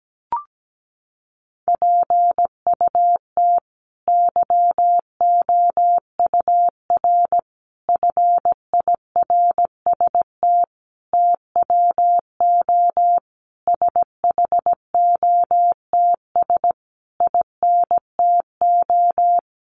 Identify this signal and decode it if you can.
Morse code, heard as a keyed tone, about 17 wpm, 700 Hz. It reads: PUT YOUR FIRST TWO SHOTS INTO
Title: A Sound Of Thunder 17wpm